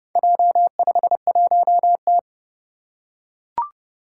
Morse code, heard as a keyed tone, about 30 wpm, 700 Hz. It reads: J51T
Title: Four Challenging Letter-Number Combinations 30wpm